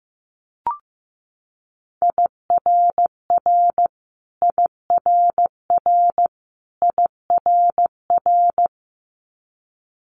Morse code, heard as a keyed tone, about 16 wpm, 700 Hz. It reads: IRR IRR IRR
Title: Multi-letter Phonemes Repeated 3x 15wpm